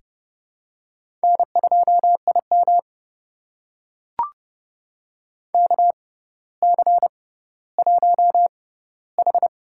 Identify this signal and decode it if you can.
Morse code, heard as a keyed tone, about 32 wpm, spaced as for 16 wpm, 700 Hz. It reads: N2IM KC1H
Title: USA Call Signs - 4 Characters - 6x Character Spacing - 30wpm